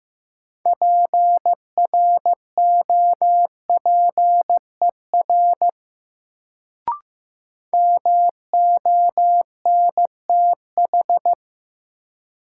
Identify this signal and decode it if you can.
Morse code, heard as a keyed tone, about 16 wpm, 700 Hz. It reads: PROPER MONTH
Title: Sentences from Top 800 Words - New Word Drill - 15wpm